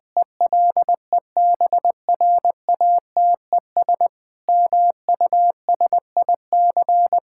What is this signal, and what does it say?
Morse code, heard as a keyed tone, about 20 wpm, 700 Hz. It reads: ELEBRATES MUSIC
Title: Sentences with 7-Word Phrases - Spoken Beginning - 20wpm